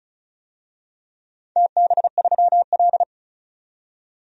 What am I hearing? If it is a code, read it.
Morse code, heard as a keyed tone, about 36 wpm, 700 Hz. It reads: TB3L
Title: DXCC-100 Call Signs - 4 Characters - 35wpm